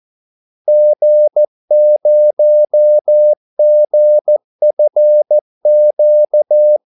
Morse code, heard as a keyed tone, about 14 wpm, 600 Hz. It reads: G0GFQ
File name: Callsigns-14wpm